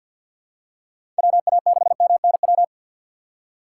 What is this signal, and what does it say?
Morse code, heard as a keyed tone, about 53 wpm, 700 Hz. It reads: WA6MNW
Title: USA Call Signs - 6 Characters - 3x Character Spacing - 50wpm